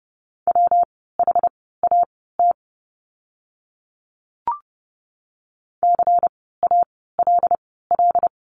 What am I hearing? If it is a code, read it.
Morse code, heard as a keyed tone, about 30 wpm, spaced as for 17 wpm, 700 Hz. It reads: WHAT CALL
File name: course / 30:10wpm / 230 Lesson 37f - Letter x - Words 30wpm